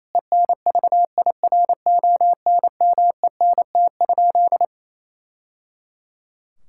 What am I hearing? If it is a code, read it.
Morse code, heard as a keyed tone, about 28 wpm, 700 Hz. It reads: ENVIRONMENT?
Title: Sentences from Top 1000 Words - Review Part 5 - 28wpm